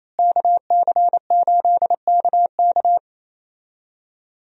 Morse code, heard as a keyed tone, about 28 wpm, 700 Hz. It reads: KC8KK
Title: USA Call Signs - 5 Characters - Repeated 2x - 28wpm